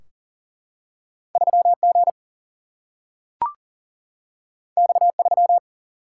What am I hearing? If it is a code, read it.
Morse code, heard as a keyed tone, about 40 wpm, 700 Hz. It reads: 3G X3
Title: Single-Letter Single-Number 40wpm